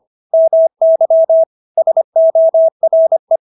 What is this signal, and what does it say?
Morse code, heard as a keyed tone, about 25 wpm, 650 Hz. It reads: MY SORE